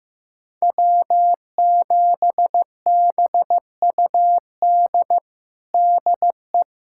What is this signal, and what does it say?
Morse code, heard as a keyed tone, about 15 wpm, 700 Hz. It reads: W7BUD DE
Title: USA Callsign DE Callsign - 5 Characters - 15wpm